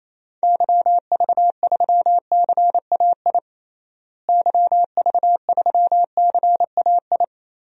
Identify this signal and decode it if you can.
Morse code, heard as a keyed tone, about 28 wpm, 700 Hz. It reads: YV3CAI YV3CAI
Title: DXCC-100 Call Signs - 6 Characters - Repeated 3x - 28wpm